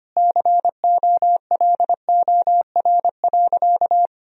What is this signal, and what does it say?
Morse code, heard as a keyed tone, about 25 wpm, 700 Hz. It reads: COLOR.
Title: Sentences from Top 1000 Words - Review Part 2 - 25wpm